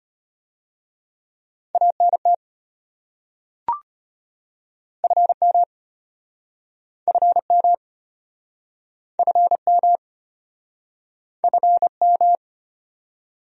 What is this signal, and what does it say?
Morse code, heard as a keyed tone, about 34 wpm, 700 Hz. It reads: ANT FM FM FM FM